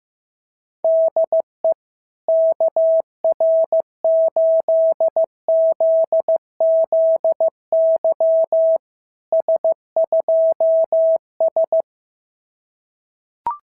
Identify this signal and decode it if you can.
Morse code, heard as a keyed tone, about 15 wpm, 650 Hz. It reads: DE KR8ZZY S2S
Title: SOTA QSO Practice 15wpm